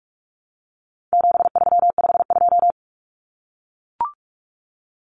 Morse code, heard as a keyed tone, about 44 wpm, 700 Hz. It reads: 7352
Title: Numbers - 4-Digit 45wpm